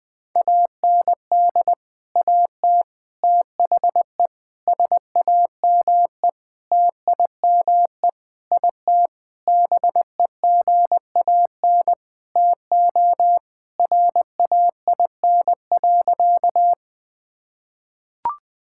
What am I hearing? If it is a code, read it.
Morse code, heard as a keyed tone, about 20 wpm, 700 Hz. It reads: AND AT THE SAME TIME IT BEGAN TO RAIN.
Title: Sentences with 5-Word Phrases - 20wpm